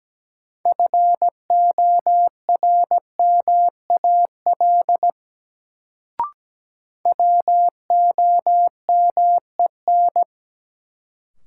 Morse code, heard as a keyed tone, about 17 wpm, 700 Hz. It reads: FORMAL WOMEN
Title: Sentences from Top 1300 Words - New Word Drill - 17wpm